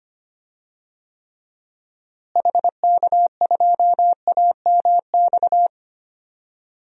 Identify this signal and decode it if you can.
Morse code, heard as a keyed tone, about 25 wpm, 700 Hz. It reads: HK2AMX
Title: DXCC-100 Call Signs - 6 Characters - Repeated 2x - 25wpm